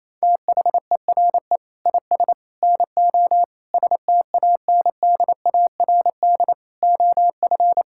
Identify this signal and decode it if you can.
Morse code, heard as a keyed tone, about 28 wpm, 700 Hz. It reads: THERE IS NO STANDARD OF
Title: Sentences from Top 1600 Words 28wpm